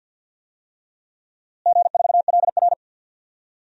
Morse code, heard as a keyed tone, about 50 wpm, 700 Hz. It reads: G4LR
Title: DXCC-100 Call Signs - 4 Characters - Repeated 2x - 50wpm